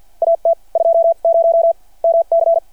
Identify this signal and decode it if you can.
Morse code, heard as a keyed tone, about 53 wpm, spaced as for 43 wpm, 670 Hz. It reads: AT 20 MC
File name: archived / CRSnet